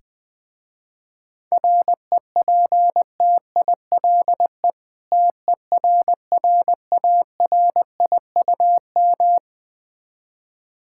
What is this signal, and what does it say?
Morse code, heard as a keyed tone, about 20 wpm, 700 Hz. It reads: REPTILE TERRARIUM
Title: Difficult Words with Hints - 20wpm